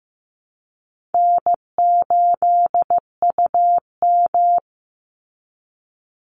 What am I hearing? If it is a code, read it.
Morse code, heard as a keyed tone, about 15 wpm, 700 Hz. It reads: N8UM
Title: Call Signs Repeated 3x - 15wpm